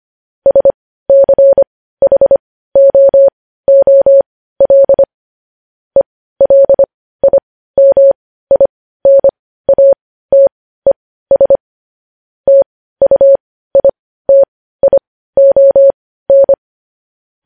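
Morse code, heard as a keyed tone, about 25 wpm, spaced as for 15 wpm, 550 Hz. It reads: SCHOOL ELIMINATES TUITION
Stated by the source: Morse code news headlines at 15 WPM.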